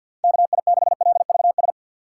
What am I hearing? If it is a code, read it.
Morse code, heard as a keyed tone, about 52 wpm, 690 Hz. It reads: KI6LVS
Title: USA Call Signs - 6 Characters - 3x Character Spacing - 50wpm